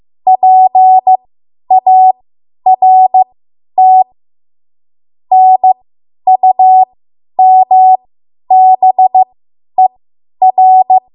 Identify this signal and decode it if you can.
Morse code, heard as a keyed tone, about 15 wpm, spaced as for 10 wpm, 750 Hz. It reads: PART NUMBER
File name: Morse / 10WPM